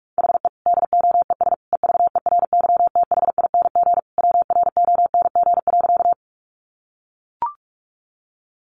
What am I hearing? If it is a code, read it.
Morse code, heard as a keyed tone, about 44 wpm, 700 Hz. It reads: HE DOES EVERYTHING WRONG.
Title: Sentences from Top 900 Words 45wpm